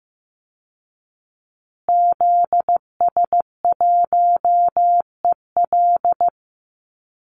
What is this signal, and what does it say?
Morse code, heard as a keyed tone, about 15 wpm, 700 Hz. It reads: ZS1EL
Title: Call Signs Repeated 2x - 15wpm